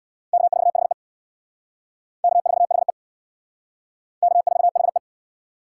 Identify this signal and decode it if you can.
Morse code, heard as a keyed tone, about 63 wpm, 700 Hz. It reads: K4HE K4HE K4HE